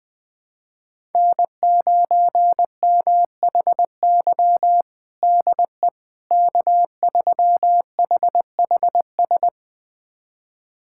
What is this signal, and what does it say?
Morse code, heard as a keyed tone, about 21 wpm, 700 Hz. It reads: N9MHY DE K3HHS
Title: USA Callsign DE Callsign - 5 Characters - 20wpm